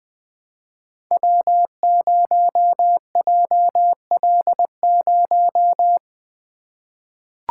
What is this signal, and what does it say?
Morse code, heard as a keyed tone, about 20 wpm, 700 Hz. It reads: W0JL0
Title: Code Groups - 5 Letter:Numbers - 20wpm